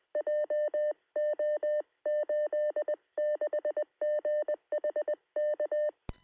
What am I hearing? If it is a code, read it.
Morse code, heard as a keyed tone, about 20 wpm, 600 Hz. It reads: JO86GH K
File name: sm7lcb / 2122z g3xdy 13cm